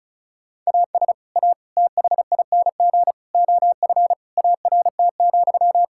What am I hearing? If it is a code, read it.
Morse code, heard as a keyed tone, about 36 wpm, 700 Hz. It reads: AS A THING OF ART,
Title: Sentences from Top 1000 Words - Review Part 6 - 35wpm